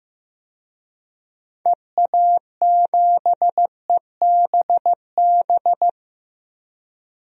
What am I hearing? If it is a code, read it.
Morse code, heard as a keyed tone, about 15 wpm, 700 Hz. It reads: EA7EBB
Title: DXCC-100 Call Signs - 6 Characters - 15wpm